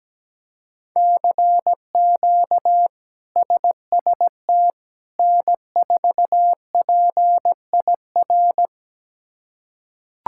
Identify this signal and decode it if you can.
Morse code, heard as a keyed tone, about 17 wpm, 700 Hz. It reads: CQ SST N4PIR
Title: K1USN SST QSO Practice - 17wpm